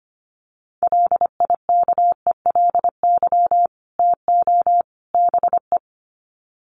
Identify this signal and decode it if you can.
Morse code, heard as a keyed tone, about 25 wpm, 700 Hz. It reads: LIKELY TO BE
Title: Sets of 3 Words 25wpm